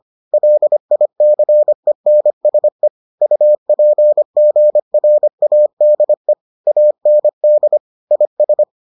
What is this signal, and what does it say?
Morse code, heard as a keyed tone, about 25 wpm, 600 Hz. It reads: LICENSE UPGRADE AND IS